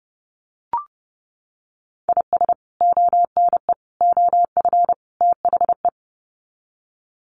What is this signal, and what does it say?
Morse code, heard as a keyed tone, about 30 wpm, 700 Hz. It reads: IS ONE OF THE
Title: Sets of 4 Words 30wpm